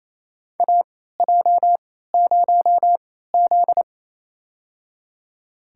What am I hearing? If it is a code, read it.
Morse code, heard as a keyed tone, about 28 wpm, spaced as for 16 wpm, 700 Hz. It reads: AJ0Z
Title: USA Call Signs - 4 Characters - 3x Character Spacing - 28wpm